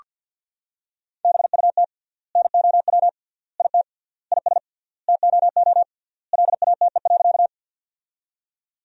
Morse code, heard as a keyed tone, about 48 wpm, 700 Hz. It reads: BUT NOW IT IS TOO LATE.
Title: 3x Word-Spacing - Sentences from Top 300 Words - 50wpm